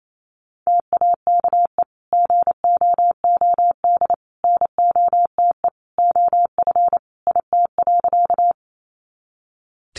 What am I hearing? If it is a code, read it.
Morse code, heard as a keyed tone, about 28 wpm, 700 Hz. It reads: TAKE GOOD NOTE OF IT.
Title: Sentences from Top 500 Words 28wpm